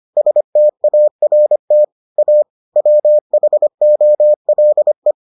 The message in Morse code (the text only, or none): START A WHOLE